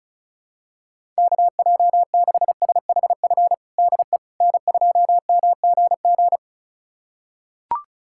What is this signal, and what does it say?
Morse code, heard as a keyed tone, about 36 wpm, 700 Hz. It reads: KJ6SHF DE N2MGG
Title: USA Callsign DE Callsign 35wpm